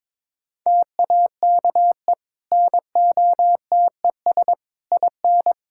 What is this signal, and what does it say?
Morse code, heard as a keyed tone, about 21 wpm, 700 Hz. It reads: TAKE NOTES IN
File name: practice / 22wpm / Easy Sentences with Hints - 22wpm